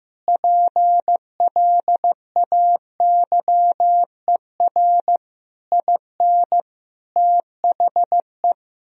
Morse code, heard as a keyed tone, about 16 wpm, 700 Hz. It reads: PLAYER IN THE
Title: Sentences from Top 1000 Words - Review Part 4 - 15wpm